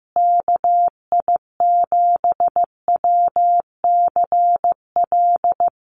Morse code, heard as a keyed tone, about 15 wpm, 700 Hz. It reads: KI7WCL
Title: USA Call Signs Repeated 3x 15wpm